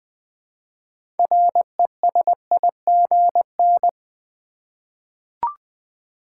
Morse code, heard as a keyed tone, about 20 wpm, 700 Hz. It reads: RESIGN
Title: 6 Letter Words - First 3 Letters Given - 20wpm